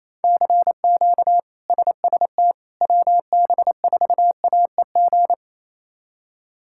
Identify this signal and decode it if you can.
Morse code, heard as a keyed tone, about 28 wpm, 700 Hz. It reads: CQ SST WB4AEG
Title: K1USN SST QSO Practice - 28wpm